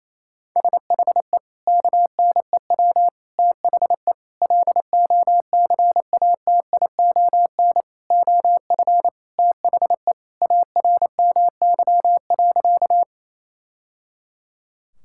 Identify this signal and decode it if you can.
Morse code, heard as a keyed tone, about 28 wpm, 700 Hz. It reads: SHE KNEW THE LOCATION OF THE ARMY.
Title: Sentences from Top 1000 Words - Review Part 6 - 28wpm